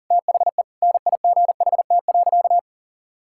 Morse code, heard as a keyed tone, about 40 wpm, 700 Hz. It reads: THE NIGHT.
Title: Sentences from Top 1000 Words - Review Part 6 - 40wpm